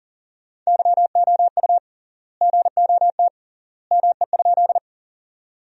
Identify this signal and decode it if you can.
Morse code, heard as a keyed tone, about 40 wpm, 700 Hz. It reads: YOU GOT ME?
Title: 3x Word-Spacing - Sentences from Top 300 Words - 40wpm